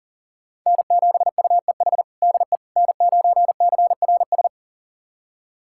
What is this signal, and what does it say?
Morse code, heard as a keyed tone, about 37 wpm, 700 Hz. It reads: N7UEH DE N9CRS
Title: USA Callsign DE Callsign - 5 Characters - 40wpm